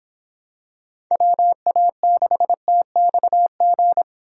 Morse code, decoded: WA6TXG

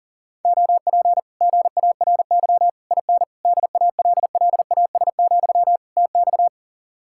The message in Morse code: OP GARY IN DALLAS, TX

